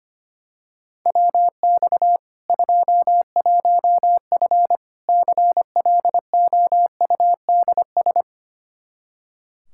Morse code, heard as a keyed tone, about 25 wpm, 700 Hz. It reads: WX 21F CLOUDS
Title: K1USN SST QSO Practice 25wpm